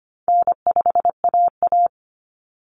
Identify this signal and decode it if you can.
Morse code, heard as a keyed tone, about 25 wpm, 700 Hz. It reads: N5AA
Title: Common Cut Numbers - 4-Digit 25wpm